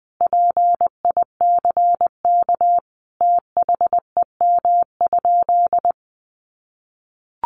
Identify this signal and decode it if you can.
Morse code, heard as a keyed tone, about 20 wpm, 700 Hz. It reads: PICK THEM?
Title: Sentences from Top 1000 Words 20wpm